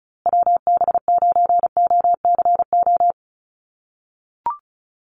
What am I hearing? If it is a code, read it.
Morse code, heard as a keyed tone, about 34 wpm, 700 Hz. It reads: WB9OCO
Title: USA Call Signs Repeated 3x 35wpm